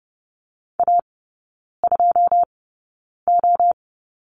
Morse code, heard as a keyed tone, about 30 wpm, spaced as for 9 wpm, 700 Hz. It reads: A2O
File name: 081 Lesson 14e - Number 2 - Three Characters 30wpm